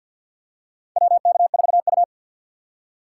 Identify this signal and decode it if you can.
Morse code, heard as a keyed tone, about 51 wpm, 700 Hz. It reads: WK4U
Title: USA Call Signs - 3x Character Spacing - 50wpm